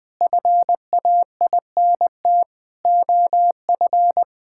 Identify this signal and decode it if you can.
Morse code, heard as a keyed tone, about 20 wpm, 700 Hz. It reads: FAINT OF
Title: Sentences with 7-Word Phrases - 20wpm